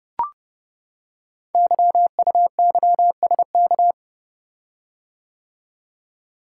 Morse code, heard as a keyed tone, about 30 wpm, 690 Hz. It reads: YUYSK
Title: Code Groups - 5 Letters - 30wpm